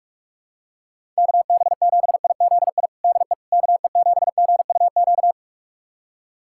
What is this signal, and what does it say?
Morse code, heard as a keyed tone, about 46 wpm, 700 Hz. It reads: KB7IZI DE KE7GUQ
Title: USA Callsign DE Callsign - 6 Characters - 45wpm